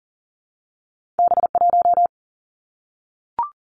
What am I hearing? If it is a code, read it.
Morse code, heard as a keyed tone, about 41 wpm, 700 Hz. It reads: B1
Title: Two Challenging Letter-Number Combinations 40wpm